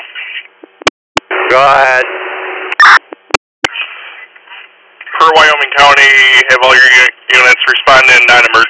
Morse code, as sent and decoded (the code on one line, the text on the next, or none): none